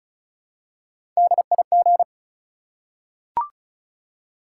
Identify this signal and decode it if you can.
Morse code, heard as a keyed tone, about 36 wpm, 700 Hz. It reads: DIG
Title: Three Letter Words - 35wpm